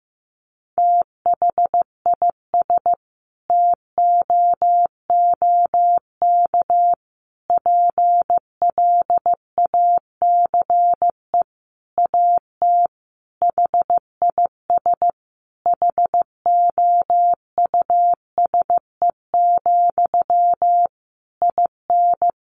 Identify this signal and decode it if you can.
Morse code, heard as a keyed tone, about 15 wpm, 700 Hz. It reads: THIS TOOK PLACE AT HIS HOUSE, IN
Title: Sentences from Top 300 Words 15wpm